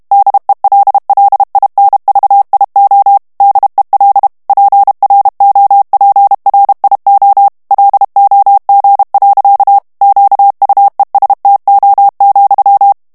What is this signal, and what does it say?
Morse code, heard as a keyed tone, about 32 wpm, 800 Hz. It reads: DELLINVIO DEL PROPRIO LOG. QUESTO,
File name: ITALIEN texte clair 2 vitesse 30 mots minute